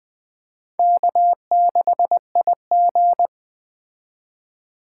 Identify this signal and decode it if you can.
Morse code, heard as a keyed tone, about 20 wpm, 700 Hz. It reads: K6IG
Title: Speed Racing - USA 1x2 Call Signs 17wpm